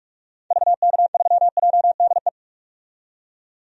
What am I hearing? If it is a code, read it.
Morse code, heard as a keyed tone, about 46 wpm, 700 Hz. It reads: VK3JDE
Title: DXCC-100 Call Signs - 6 Characters - 45wpm